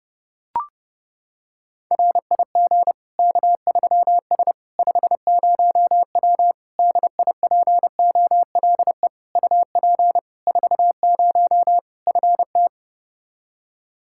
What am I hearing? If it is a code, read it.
Morse code, heard as a keyed tone, about 32 wpm, 700 Hz. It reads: RIG K3S 50W DIPOLE UP 40 FT
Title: K1USN SST QSO Practice 30wpm